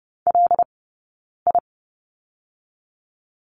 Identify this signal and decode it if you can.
Morse code, heard as a keyed tone, about 30 wpm, 700 Hz. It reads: L I
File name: 092 Lesson 16d - Letter c - Two Characters 30wpm